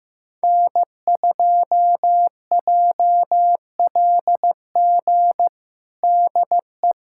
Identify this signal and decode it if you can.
Morse code, heard as a keyed tone, about 16 wpm, 700 Hz. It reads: N2JLG DE
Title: USA Callsign DE Callsign 15wpm